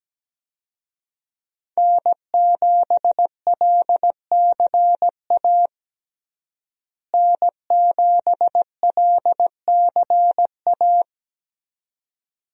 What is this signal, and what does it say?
Morse code, heard as a keyed tone, about 17 wpm, 700 Hz. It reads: N7LCA N7LCA
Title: USA Call Signs - 5 Characters - Repeated 2x - 17wpm